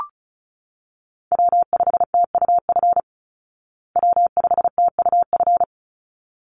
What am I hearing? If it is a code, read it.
Morse code, heard as a keyed tone, about 34 wpm, 700 Hz. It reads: W5TUF W5TUF
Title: Call Signs Repeated 3x - 35wpm